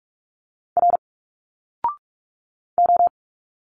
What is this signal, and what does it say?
Morse code, heard as a keyed tone, about 46 wpm, 700 Hz. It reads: R O
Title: Single Letters 45wpm